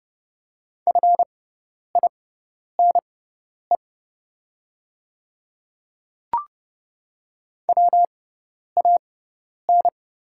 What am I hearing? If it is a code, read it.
Morse code, heard as a keyed tone, about 30 wpm, spaced as for 10 wpm, 690 Hz. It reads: FINE WAN
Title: 6x Character Spacing - Top 500 Words - 30wpm